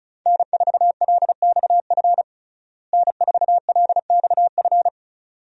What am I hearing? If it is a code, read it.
Morse code, heard as a keyed tone, about 34 wpm, 700 Hz. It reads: N4LXF N4LXF
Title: USA Call Signs - 5 Characters - Repeated 2x - 35wpm